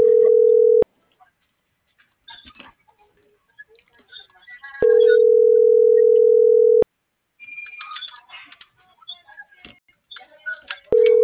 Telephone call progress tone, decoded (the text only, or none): ringback tone